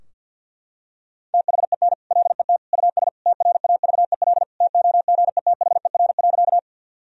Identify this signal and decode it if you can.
Morse code, heard as a keyed tone, about 52 wpm, 700 Hz. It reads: THEN LET US TRAVEL TOGETHER.
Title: Sentences from Top 1000 Words - Review Part 2 - 50wpm